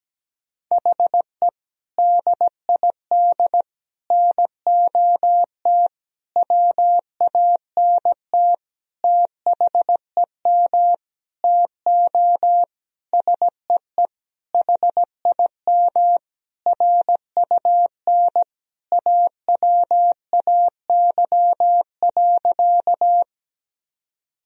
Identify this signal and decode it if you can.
Morse code, heard as a keyed tone, about 17 wpm, 700 Hz. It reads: HE DID NOT WANT THEM TO SEE HIM RUN AWAY.
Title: Sentences from Top 1000 Words - Review Part 1 - 17wpm